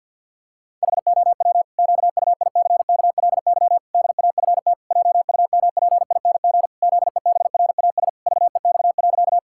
Morse code, heard as a keyed tone, about 50 wpm, 700 Hz. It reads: HOW QUICKLY DAFT JUMPING ZEBRAS VEX.
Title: Pangram Challenge v2 - 50wpm